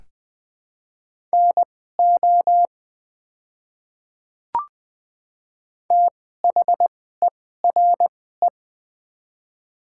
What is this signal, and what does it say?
Morse code, heard as a keyed tone, about 20 wpm, spaced as for 14 wpm, 700 Hz. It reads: NO THERE
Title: 2x Character Spacing - Top 100 Words - 20wpm